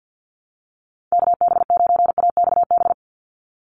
Morse code, heard as a keyed tone, about 50 wpm, 700 Hz. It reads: KB9AXB
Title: USA Call Signs 50wpm